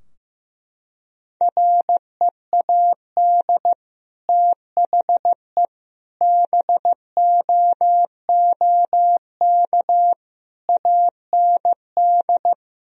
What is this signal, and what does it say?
Morse code, heard as a keyed tone, about 15 wpm, 700 Hz. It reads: READ THE BOOK AND
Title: Sentences from Top 1000 Words - Review Part 3 - 15wpm